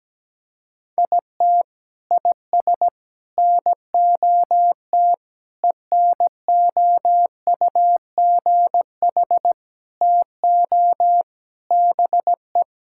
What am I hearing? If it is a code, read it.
Morse code, heard as a keyed tone, about 17 wpm, 700 Hz. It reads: IT IS NOT ENOUGH TO BE
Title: Sentences from Top 1000 Words - Review Part 6 - 17wpm